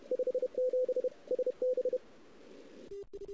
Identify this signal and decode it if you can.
Morse code, heard as a keyed tone, about 31 wpm, 510 Hz. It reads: 57 SB
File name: CW / 2008-ss-cw-k5tr-op-n5rz -02:00